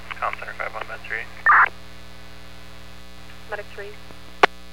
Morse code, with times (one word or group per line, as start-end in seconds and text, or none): none